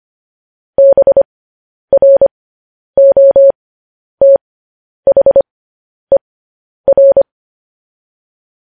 Morse code, heard as a keyed tone, about 25 wpm, spaced as for 10 wpm, 550 Hz. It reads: BROTHER
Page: news headlines in morse code at 10 wpm